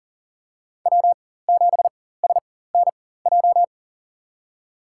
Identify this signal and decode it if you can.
Morse code, heard as a keyed tone, about 38 wpm, spaced as for 18 wpm, 700 Hz. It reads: W7SNJ
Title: USA Call Signs - 5 Characters - 4x Character Spacing - 40wpm